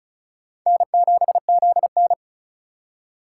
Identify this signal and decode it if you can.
Morse code, heard as a keyed tone, about 36 wpm, 700 Hz. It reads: N7ZN
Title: K1USN SST QSO Practice - 35wpm